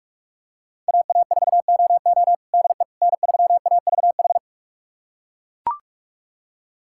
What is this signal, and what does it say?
Morse code, heard as a keyed tone, about 41 wpm, 700 Hz. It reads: AA4OO DE N3AVH
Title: USA Callsign DE Callsign - 5 Characters - 45wpm